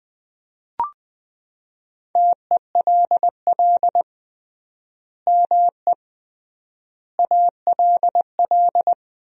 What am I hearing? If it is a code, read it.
Morse code, heard as a keyed tone, about 21 wpm, 700 Hz. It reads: TELL ME ALL
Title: 3x Word-Spacing - Sentences from Top 200 Words - 20wpm